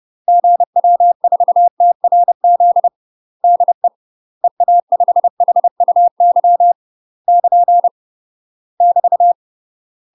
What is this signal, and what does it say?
Morse code, heard as a keyed tone, about 30 wpm, 700 Hz. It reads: GW4TRZ DE EA5HUY <KN> <BT>
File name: UK MASTER QSOs TRAINING / 30WPM / UK MASTER QSOs 0030